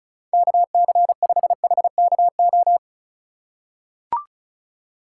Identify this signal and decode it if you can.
Morse code, heard as a keyed tone, about 36 wpm, 700 Hz. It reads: KC5HKO
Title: USA Call Signs - 6 Characters - 3x Character Spacing - 35wpm